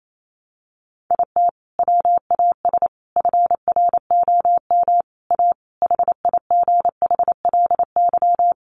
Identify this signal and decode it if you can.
Morse code, heard as a keyed tone, about 28 wpm, 700 Hz. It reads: IT WAS FROM A HIGHLY
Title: Sentences from Top 1100 Words 28wpm